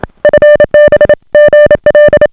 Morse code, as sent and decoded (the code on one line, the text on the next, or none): ..-. -... / --. .-..
FB GL